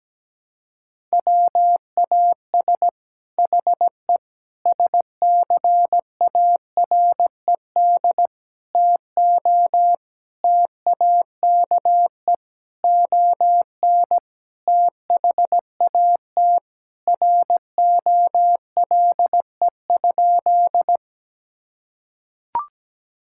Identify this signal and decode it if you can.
Morse code, heard as a keyed tone, about 17 wpm, 700 Hz. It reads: WAS HE SCARED TO TAKE ON THAT ROLE?
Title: Sentences from Top 1000 Words - Review Part 6 - 17wpm